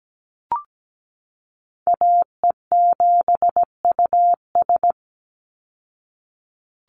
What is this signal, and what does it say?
Morse code, heard as a keyed tone, about 17 wpm, 700 Hz. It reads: AE7US
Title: Call Signs Repeated 3x - 17wpm